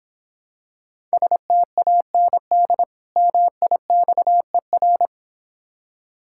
Morse code, heard as a keyed tone, about 26 wpm, 700 Hz. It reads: STAND MIXER